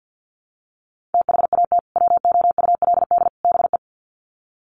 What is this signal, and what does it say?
Morse code, heard as a keyed tone, about 48 wpm, 700 Hz. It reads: THAT WOULD BE
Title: Sets of 3 Words 50wpm